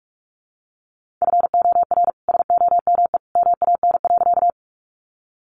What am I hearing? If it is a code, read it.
Morse code, heard as a keyed tone, about 44 wpm, 700 Hz. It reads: FOR SOME MAN.
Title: Sentences from Top 200 Words 45wpm